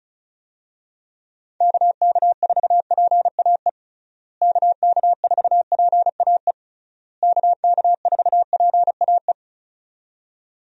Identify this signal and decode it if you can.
Morse code, heard as a keyed tone, about 34 wpm, 700 Hz. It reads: KK4PAE KK4PAE KK4PAE
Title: USA Call Signs - 6 Characters - Repeated 3x - 35wpm